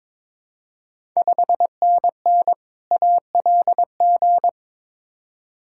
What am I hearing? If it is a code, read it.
Morse code, heard as a keyed tone, about 22 wpm, 700 Hz. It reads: 5NN ALG
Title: Ontario QSO Party Exchange 22wpm